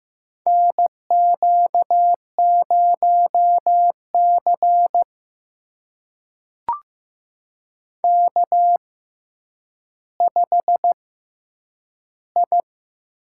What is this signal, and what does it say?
Morse code, heard as a keyed tone, about 15 wpm, spaced as for 9 wpm, 700 Hz. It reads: NQ0C K5I
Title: USA Call Signs - 4 Characters - 6x Character Spacing - 15wpm